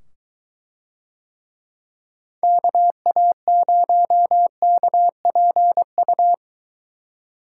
Morse code, heard as a keyed tone, about 23 wpm, 700 Hz. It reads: KA0KPU